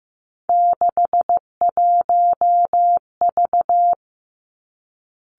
Morse code, heard as a keyed tone, about 15 wpm, 700 Hz. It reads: 61V
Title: Three Challenging Letter-Number Combinations 15wpm